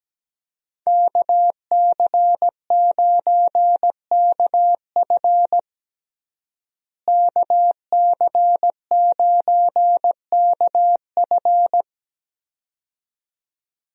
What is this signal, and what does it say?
Morse code, heard as a keyed tone, about 17 wpm, 700 Hz. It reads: KC9KF KC9KF
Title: USA Call Signs - 5 Characters - Repeated 2x - 17wpm